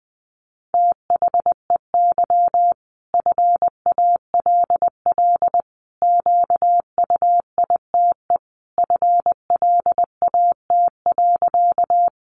Morse code, heard as a keyed tone, about 20 wpm, 700 Hz. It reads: THEY FALL QUITE FLAT.
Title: Sentences from Top 1000 Words 20wpm